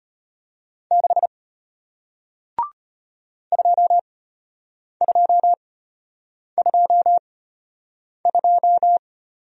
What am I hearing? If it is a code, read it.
Morse code, heard as a keyed tone, about 33 wpm, spaced as for 8 wpm, 700 Hz. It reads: 6 2222